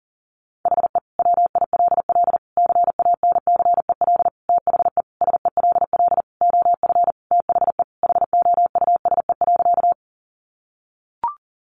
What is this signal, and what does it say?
Morse code, heard as a keyed tone, about 40 wpm, 700 Hz. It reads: HE WILL CANCEL THE SELL OF THE HOUSE.
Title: Sentences from Top 1200 Words 40wpm